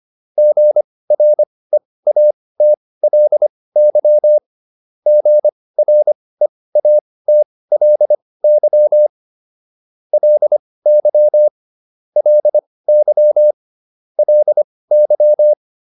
Morse code, heard as a keyed tone, about 25 wpm, spaced as for 18 wpm, 600 Hz. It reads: GREATLY GREATLY LY LY LY